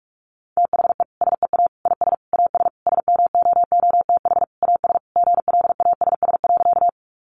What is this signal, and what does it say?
Morse code, heard as a keyed tone, about 44 wpm, 700 Hz. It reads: THE SEA IS AS SMOOTH AS GLASS.
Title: Sentences from Top 1400 Words 45wpm